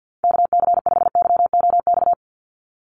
Morse code, heard as a keyed tone, about 50 wpm, 700 Hz. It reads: KC5YOX
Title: USA Call Signs 50wpm